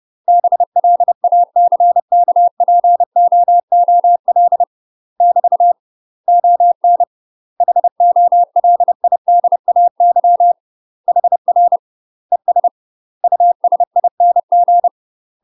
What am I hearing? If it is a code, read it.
Morse code, heard as a keyed tone, about 30 wpm, 700 Hz. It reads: BLACKPOOL <BT> ON HOLIDAY HR ES USING
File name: UK MASTER QSOs TRAINING / 30WPM / UK MASTER QSOs 0044